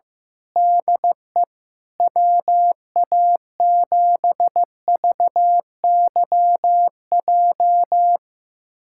Morse code, heard as a keyed tone, about 15 wpm, 700 Hz. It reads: DE WA7VYJ
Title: USA Callsign DE Callsign 15wpm